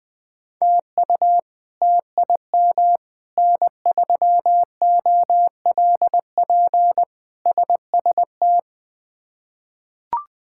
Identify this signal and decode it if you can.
Morse code, heard as a keyed tone, about 20 wpm, 700 Hz. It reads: TU TIM N3OLP SST
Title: K1USN SST QSO Practice - 20wpm